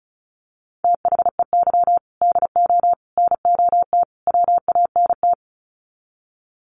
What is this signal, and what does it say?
Morse code, heard as a keyed tone, about 34 wpm, 700 Hz. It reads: THEY DO NOT WANT
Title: Sets of 4 Words 35wpm